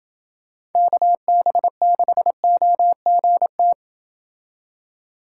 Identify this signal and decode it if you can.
Morse code, heard as a keyed tone, about 28 wpm, 700 Hz. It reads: KB6OGT